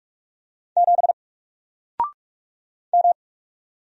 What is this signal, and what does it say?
Morse code, heard as a keyed tone, about 44 wpm, 700 Hz. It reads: 7 M
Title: Single Character (Letter:Number:Punctuation:Prosign) 45wpm